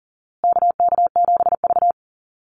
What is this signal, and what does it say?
Morse code, heard as a keyed tone, about 40 wpm, 700 Hz. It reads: KK7V
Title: USA Call Signs Repeated 2x 40wpm